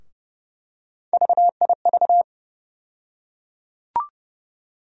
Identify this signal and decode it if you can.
Morse code, heard as a keyed tone, about 32 wpm, 700 Hz. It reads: VIV
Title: USA Call Sign Postfixes - 3 Letters 30wpm